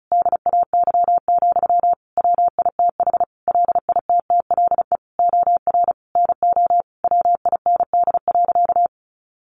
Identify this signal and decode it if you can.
Morse code, heard as a keyed tone, about 34 wpm, 700 Hz. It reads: DAY, WITH LITTLE OR NO WIND.
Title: Sentences from Top 1400 Words 35wpm